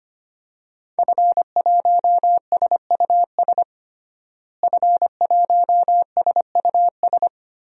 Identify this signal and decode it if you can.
Morse code, heard as a keyed tone, about 25 wpm, 700 Hz. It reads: F1SUS F1SUS
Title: DXCC-100 Call Signs - 5 Characters - Repeated 3x - 25wpm